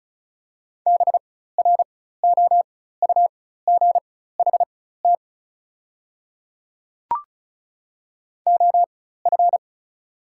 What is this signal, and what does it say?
Morse code, heard as a keyed tone, about 36 wpm, spaced as for 16 wpm, 700 Hz. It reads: BROUGHT OF
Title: 4x Character Spacing - Top 500 Words - 35wpm